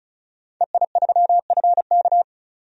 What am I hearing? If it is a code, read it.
Morse code, heard as a keyed tone, about 34 wpm, 700 Hz. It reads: EI3FK
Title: DXCC-100 Call Signs - 5 Characters - 35wpm